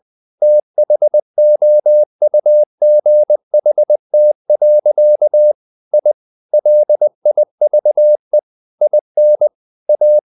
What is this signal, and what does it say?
Morse code, heard as a keyed tone, about 20 wpm, 600 Hz. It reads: THOUGHT. I LIVE IN A
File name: the old man and the sea part 2 20wpm